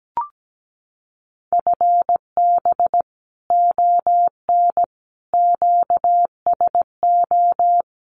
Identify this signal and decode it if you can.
Morse code, heard as a keyed tone, about 17 wpm, 700 Hz. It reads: FB ON QSO
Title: Common QSO Phrases 17wpm